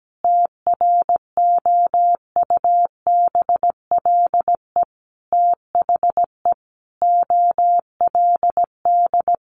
Morse code, heard as a keyed tone, about 17 wpm, 700 Hz. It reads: TROUBLE THE OLD
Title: Sentences from Top 1200 Words 17wpm